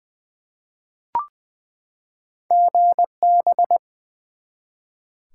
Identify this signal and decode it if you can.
Morse code, heard as a keyed tone, about 20 wpm, 700 Hz. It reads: GB
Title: Top 90 QSO Elements 20wpm